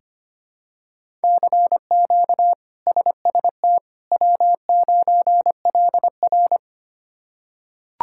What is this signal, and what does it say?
Morse code, heard as a keyed tone, about 25 wpm, 700 Hz. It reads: CQ SST W9LR
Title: K1USN SST QSO Practice - 25wpm